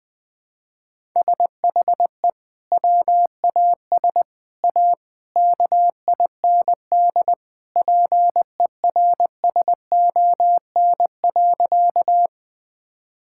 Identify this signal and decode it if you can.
Morse code, heard as a keyed tone, about 20 wpm, 700 Hz. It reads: SHE WAS A KIND PERSON.
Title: Sentences with 3-Word Phrases - 20wpm